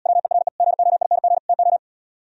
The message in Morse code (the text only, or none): WERE GOING TO